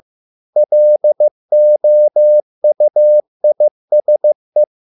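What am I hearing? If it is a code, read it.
Morse code, heard as a keyed tone, about 15 wpm, 600 Hz. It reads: LOUISE
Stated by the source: Morse code at a slow 15 wpm from CW med Gnister.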